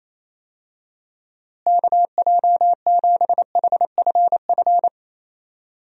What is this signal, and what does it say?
Morse code, heard as a keyed tone, about 28 wpm, 700 Hz. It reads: KJ7HFF
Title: USA Call Signs - 6 Characters - Repeated 3x - 28wpm